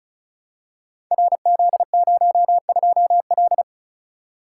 Morse code, heard as a keyed tone, about 36 wpm, 700 Hz. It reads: RZ02L
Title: Code Groups - 5 Letter:Numbers - 35wpm